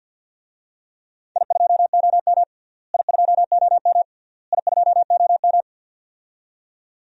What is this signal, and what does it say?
Morse code, heard as a keyed tone, about 46 wpm, 700 Hz. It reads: I2OM I2OM I2OM
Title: DXCC-100 Call Signs - 4 Characters - Repeated 3x - 50wpm